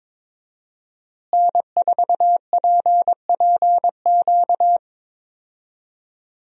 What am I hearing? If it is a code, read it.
Morse code, heard as a keyed tone, about 22 wpm, 700 Hz. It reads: N4PPQ
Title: USA Call Signs - 5 Characters - Repeated 3x - 22wpm